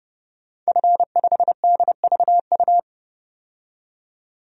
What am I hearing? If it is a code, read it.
Morse code, heard as a keyed tone, about 31 wpm, 700 Hz. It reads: F5DVU
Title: DXCC-100 Call Signs - 5 Characters - 30wpm